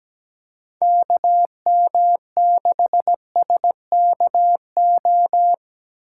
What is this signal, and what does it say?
Morse code, heard as a keyed tone, about 17 wpm, 700 Hz. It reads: KM6SKO
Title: USA Call Signs - 6 Characters - 2x Character Spacing - 17wpm